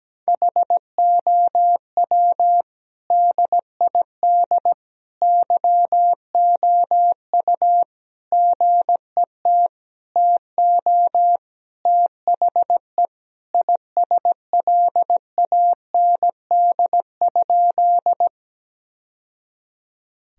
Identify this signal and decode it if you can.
Morse code, heard as a keyed tone, about 17 wpm, 700 Hz. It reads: HOW DID YOU GET TO THE ISLAND?
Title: Sentences from Top 1000 Words - Review Part 3 - 17wpm